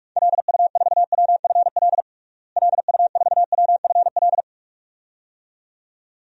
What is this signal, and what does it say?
Morse code, heard as a keyed tone, about 46 wpm, 700 Hz. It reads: LU4WFL LU4WFL
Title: DXCC-100 Call Signs - 6 Characters - Repeated 2x - 45wpm